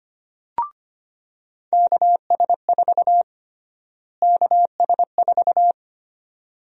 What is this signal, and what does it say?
Morse code, heard as a keyed tone, about 25 wpm, 700 Hz. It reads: KS4 KS4
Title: USA Call Sign Prefixes - Repeated 2x - 25wpm